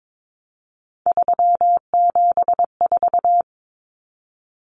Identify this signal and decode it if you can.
Morse code, heard as a keyed tone, about 22 wpm, 700 Hz. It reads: 374
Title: Numbers - 3-Digit 22wpm